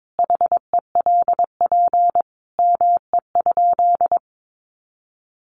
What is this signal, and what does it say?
Morse code, heard as a keyed tone, about 22 wpm, 700 Hz. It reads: HELP ME?
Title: Sentences from Top 200 Words 22wpm